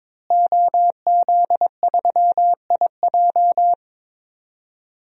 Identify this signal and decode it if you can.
Morse code, heard as a keyed tone, about 22 wpm, 700 Hz. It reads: OZ3IJ
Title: DXCC-100 Call Signs - 5 Characters - Repeated 3x - 22wpm